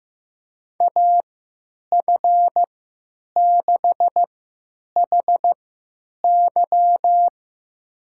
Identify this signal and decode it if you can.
Morse code, heard as a keyed tone, about 15 wpm, spaced as for 9 wpm, 700 Hz. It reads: AF6HY
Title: USA Call Signs - 3x Character Spacing - 15wpm